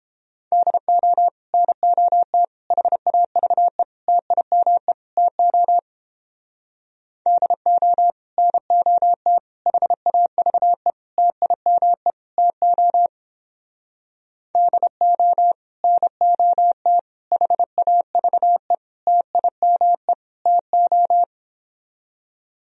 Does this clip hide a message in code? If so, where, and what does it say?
Yes: Morse code from 0.5 to 21.2 s, heard as a keyed tone, about 29 wpm, 700 Hz. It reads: DO NOT HAVE TIME TO DO NOT HAVE TIME TO DO NOT HAVE TIME TO